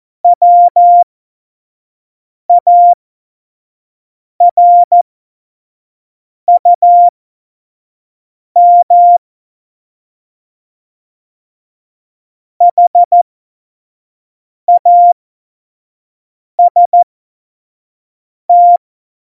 Morse code, heard as a keyed tone, about 14 wpm, spaced as for 5 wpm, 700 Hz. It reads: WARUM HAST
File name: Morsekurs / Witze / Witz 29. August 17, Tempo 25 F70